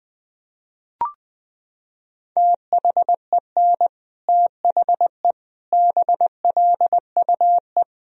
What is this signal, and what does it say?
Morse code, heard as a keyed tone, about 20 wpm, 700 Hz. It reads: THEN THE BLUE
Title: Sentences from Top 1000 Words - Review Part 3 - 20wpm